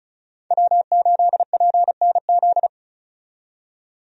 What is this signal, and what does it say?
Morse code, heard as a keyed tone, about 34 wpm, 700 Hz. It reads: W8PNZ
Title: USA Call Signs - 5 Characters - 35wpm